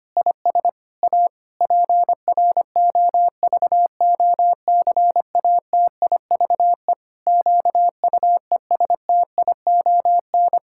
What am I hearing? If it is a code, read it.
Morse code, heard as a keyed tone, about 25 wpm, 700 Hz. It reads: IS A PROVOCATIVE QUESTION
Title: Pangram Challenge v2 - 25wpm